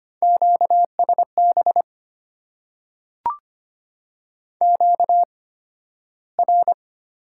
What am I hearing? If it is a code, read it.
Morse code, heard as a keyed tone, about 25 wpm, spaced as for 13 wpm, 700 Hz. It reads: QSB QR
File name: Top 90 QSO Elements - 8x Character Spacing - 25wpm